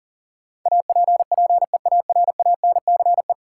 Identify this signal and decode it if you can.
Morse code, heard as a keyed tone, about 40 wpm, 700 Hz. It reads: APPEARANCE
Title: Sentences from Top 1100 Words - New Word Drill Alt - 40wpm